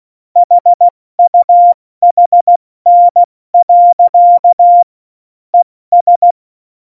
Morse code, heard as a keyed tone, about 16 wpm, 700 Hz. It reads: HUHN. ES
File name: Morsekurs / Witze / Witz 14. Mai 2019, 70 F80